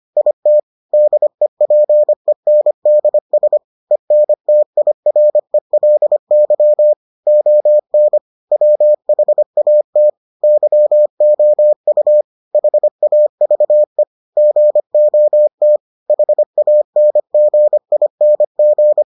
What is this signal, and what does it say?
Morse code, heard as a keyed tone, about 25 wpm, 600 Hz. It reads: IT DEPENDS ENTIRELY ON WHAT YOU HAVE GOT HANGING